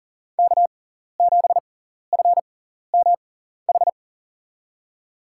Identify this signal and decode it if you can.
Morse code, heard as a keyed tone, about 40 wpm, spaced as for 14 wpm, 700 Hz. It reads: K7FMH
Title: USA Call Signs - 5 Characters - 6x Character Spacing - 40wpm